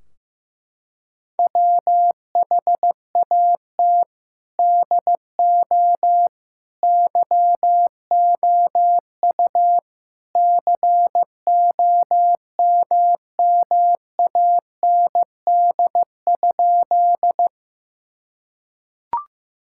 Morse code, heard as a keyed tone, about 15 wpm, 700 Hz. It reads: WHAT DO YOU COMMAND?
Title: Sentences from Top 1600 Words 15wpm